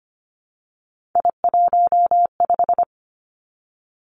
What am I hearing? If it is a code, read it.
Morse code, heard as a keyed tone, about 25 wpm, 700 Hz. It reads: I15
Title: Three Challenging Letter-Number Combinations 25wpm